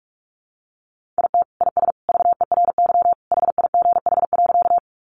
Morse code, heard as a keyed tone, about 44 wpm, 700 Hz. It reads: IT IS VERY HIGH.
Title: Sentences from Top 200 Words 45wpm